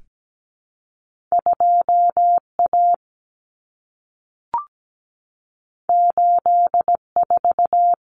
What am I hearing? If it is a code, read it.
Morse code, heard as a keyed tone, about 17 wpm, 700 Hz. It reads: 2A 84
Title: Common Cut Numbers - 2-Digit 17wpm